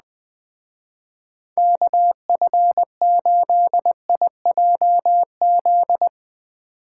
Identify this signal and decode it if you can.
Morse code, heard as a keyed tone, about 20 wpm, 700 Hz. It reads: KF8IJZ